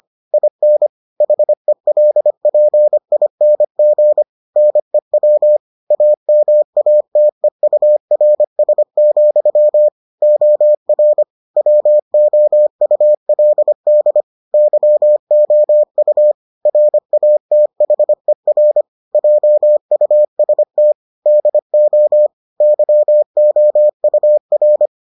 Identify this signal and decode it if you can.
Morse code, heard as a keyed tone, about 25 wpm, 600 Hz. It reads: IN HELPING NEW AMATEURS, OR WOULD YOU RATHER JUST DO YOUR